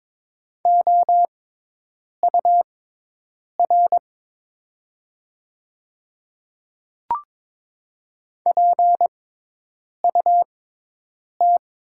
Morse code, heard as a keyed tone, about 22 wpm, spaced as for 8 wpm, 690 Hz. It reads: OUR PUT
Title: 6x Character Spacing - Top 500 Words - 22wpm